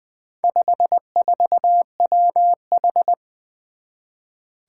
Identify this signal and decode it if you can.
Morse code, heard as a keyed tone, about 20 wpm, 700 Hz. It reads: 54WH
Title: Four Challenging Letter-Number Combinations 20wpm